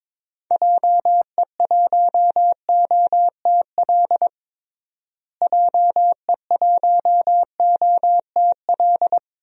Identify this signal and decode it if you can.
Morse code, heard as a keyed tone, about 22 wpm, 700 Hz. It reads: JE1OTL JE1OTL
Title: DXCC-100 Call Signs - 6 Characters - Repeated 2x - 22wpm